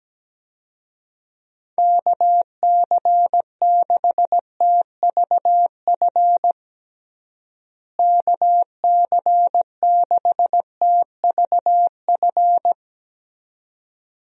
Morse code, heard as a keyed tone, about 17 wpm, 700 Hz. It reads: KC6TVF KC6TVF
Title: USA Call Signs - 6 Characters - Repeated 2x - 17wpm